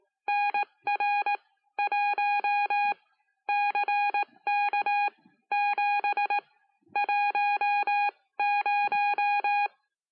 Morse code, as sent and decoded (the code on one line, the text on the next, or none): -. .-. .---- / -.-. -.- --... / .---- -----
NR1 CK7 10